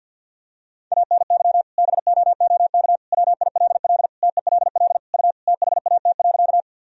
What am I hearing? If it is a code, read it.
Morse code, heard as a keyed tone, about 52 wpm, 700 Hz. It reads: ANY BOOK WILL TELL U THAT.
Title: Sentences from Top 300 Words and Abbreviations 50wpm